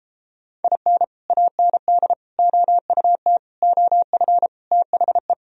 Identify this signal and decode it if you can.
Morse code, heard as a keyed tone, about 33 wpm, 700 Hz. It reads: IN AND OUT OF THE